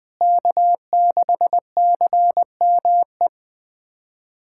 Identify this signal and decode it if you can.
Morse code, heard as a keyed tone, about 21 wpm, 700 Hz. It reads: K6CME
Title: USA Callsign DE Callsign - 5 Characters - 20wpm